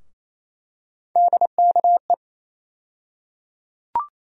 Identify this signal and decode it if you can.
Morse code, heard as a keyed tone, about 28 wpm, 700 Hz. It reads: DKE
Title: USA Call Sign Postfixes - 3 Letters 28wpm